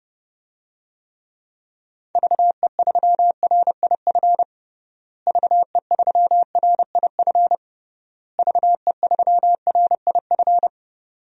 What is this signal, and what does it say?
Morse code, heard as a keyed tone, about 30 wpm, 700 Hz. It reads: VE3RIF VE3RIF VE3RIF
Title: DXCC-100 Call Signs - 6 Characters - Repeated 3x - 30wpm